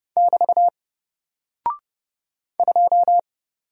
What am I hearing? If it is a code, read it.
Morse code, heard as a keyed tone, about 30 wpm, 700 Hz. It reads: <BT> 2
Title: Single Character (Letter:Number:Punctuation:Prosign) 30wpm